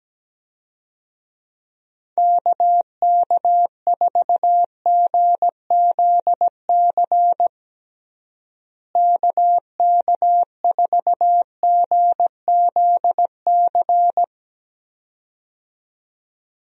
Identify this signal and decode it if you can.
Morse code, heard as a keyed tone, about 17 wpm, 700 Hz. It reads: KK4GZC KK4GZC
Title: USA Call Signs - 6 Characters - Repeated 2x - 17wpm